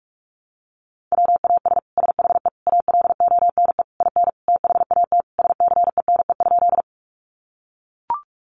Morse code, heard as a keyed tone, about 44 wpm, 700 Hz. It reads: WAS SHE ALONE IN THAT SCENE?
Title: Sentences from Top 1100 Words 45wpm